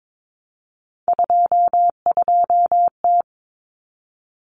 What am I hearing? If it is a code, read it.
Morse code, heard as a keyed tone, about 22 wpm, 700 Hz. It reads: 22T
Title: Common Cut Numbers - 3-Digit 22wpm